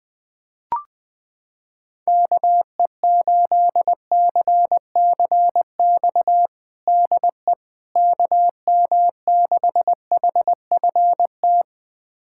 Morse code, heard as a keyed tone, about 20 wpm, 700 Hz. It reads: KE8CCX DE KM6HFT
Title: USA Callsign DE Callsign - 6 Characters - 20wpm